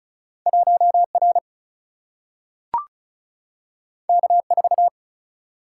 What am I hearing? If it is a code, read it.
Morse code, heard as a keyed tone, about 36 wpm, 700 Hz. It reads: 1 R K 4
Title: Single-Letter Single-Number 35wpm